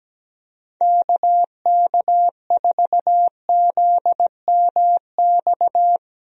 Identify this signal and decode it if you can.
Morse code, heard as a keyed tone, about 17 wpm, 700 Hz. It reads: KK4ZMX
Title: USA Call Signs - 6 Characters - Repeated 2x - 17wpm